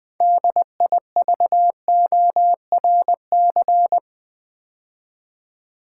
Morse code, heard as a keyed tone, about 20 wpm, 700 Hz. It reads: DIVORC
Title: Build A Word - 3 Letters At A Time - 8-Letter Words - 20wpm